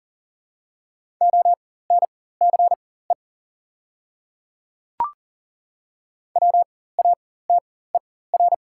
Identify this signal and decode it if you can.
Morse code, heard as a keyed tone, about 37 wpm, spaced as for 18 wpm, 700 Hz. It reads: ONCE WATER
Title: 4x Character Spacing - Top 300 Words - 40wpm